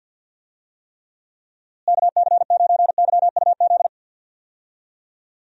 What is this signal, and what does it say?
Morse code, heard as a keyed tone, about 50 wpm, 700 Hz. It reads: KC9YUZ
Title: USA Call Signs - 6 Characters - Repeated 2x - 50wpm